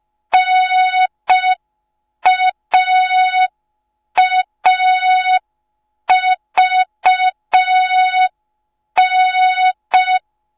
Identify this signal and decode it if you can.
Morse code, heard as a keyed tone, about 5 wpm, 3700 Hz. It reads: NAAVN